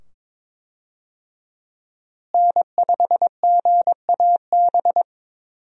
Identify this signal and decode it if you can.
Morse code, heard as a keyed tone, about 22 wpm, 700 Hz. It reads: N5GAB
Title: USA Call Signs - 5 Characters - Repeated 2x - 22wpm